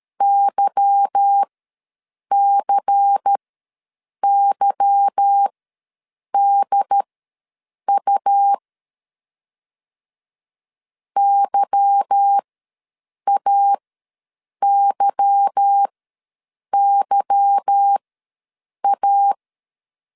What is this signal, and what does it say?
Morse code, heard as a keyed tone, about 13 wpm, spaced as for 7 wpm, 800 Hz. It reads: YCYDU YAYYA
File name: LEZIONI / 8 YJ45 / 8 Y ws 12wpm 3409